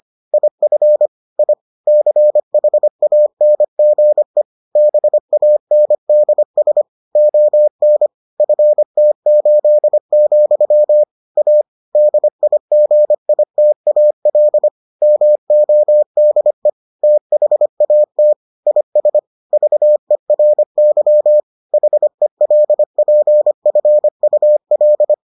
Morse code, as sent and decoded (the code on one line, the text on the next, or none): .. ..-. / .. / -.-. .... .- -. --. / -... .- -. -.. ... / --- -. / ..-. - ---.. --..-- / .- / -.. .. --. .. - .- .-.. / -- --- -.. / - .... .- - / .. ... / ...- . .-. -.-- / .... .-.. .--. ..-. ..- .-..
IF I CHANGE BANDS ON FT8, A DIGITAL MODE THAT IS VERY HELPFUL